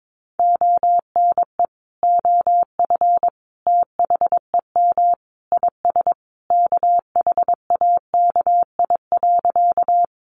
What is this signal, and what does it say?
Morse code, heard as a keyed tone, about 22 wpm, 700 Hz. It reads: ONE OF THEM IS KHAKI.
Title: Additional Easy Phrases 22wpm